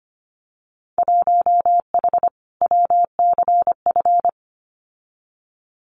Morse code, heard as a keyed tone, about 25 wpm, 700 Hz. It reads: 1H WCF
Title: Winter Field Day Exchanges 25wpm